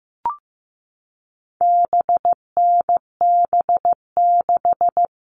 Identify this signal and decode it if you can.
Morse code, heard as a keyed tone, about 15 wpm, 700 Hz. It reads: BNB6
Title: All Cut Numbers - 4-Digit 15wpm